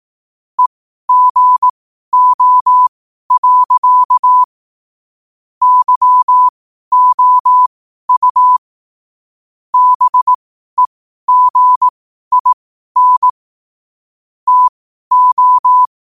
Morse code, heard as a keyed tone, about 18 wpm, spaced as for 12 wpm, 1000 Hz. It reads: EGO. YOU BEGIN TO
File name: qotdfiles / 12wpm